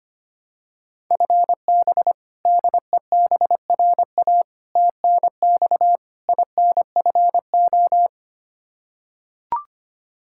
Morse code, heard as a keyed tone, about 25 wpm, 700 Hz. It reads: FB DEBRA TNX INFO
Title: Practice QSO - Exchange 3 - More Info - 25wpm